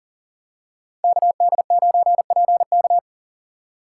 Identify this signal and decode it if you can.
Morse code, heard as a keyed tone, about 38 wpm, 700 Hz. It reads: KD9PK
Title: USA Call Signs - 5 Characters - 2x Character Spacing - 40wpm